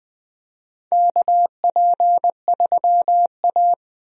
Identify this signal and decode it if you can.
Morse code, heard as a keyed tone, about 20 wpm, 700 Hz. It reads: KP3A
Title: USA Call Signs - 4 Characters - Repeated 2x - 20wpm